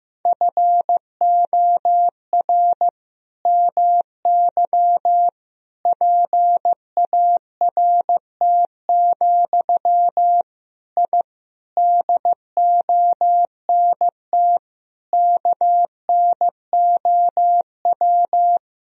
Morse code, heard as a keyed tone, about 15 wpm, 700 Hz. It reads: FOR MY PART, I DONT KNOW
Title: Sentences from Top 1000 Words - Review Part 1 - 15wpm